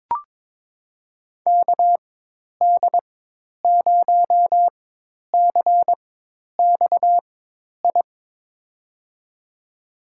Morse code, heard as a keyed tone, about 22 wpm, spaced as for 10 wpm, 700 Hz. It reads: KD0CXI
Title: USA Call Signs - 4x Character Spacing - 22wpm